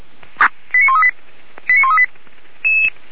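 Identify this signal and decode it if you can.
Five-tone selective call (ZVEI), heard as quick runs of five tones, tones of about 70 ms. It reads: 89138 89138
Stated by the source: ZVEI five-tone selective call received by SDR on 464 MHz, from a fire brigade's archive.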